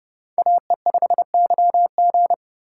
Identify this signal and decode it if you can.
Morse code, heard as a keyed tone, about 32 wpm, 700 Hz. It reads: AE5YG